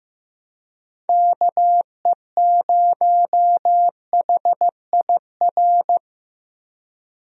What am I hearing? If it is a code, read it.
Morse code, heard as a keyed tone, about 16 wpm, 700 Hz. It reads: KE0HIR
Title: USA Call Signs - 6 Characters - Repeated 3x - 15wpm